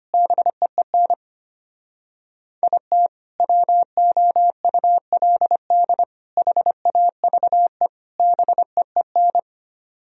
Morse code, heard as a keyed tone, about 24 wpm, 700 Hz. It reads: BEEN IT WOULD HAVE BEEN
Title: Speed-Racing - Sets of 4 Words 25wpm